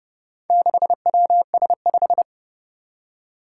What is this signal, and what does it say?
Morse code, heard as a keyed tone, about 31 wpm, 700 Hz. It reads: 6WS5
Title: Four Challenging Letter-Number Combinations 30wpm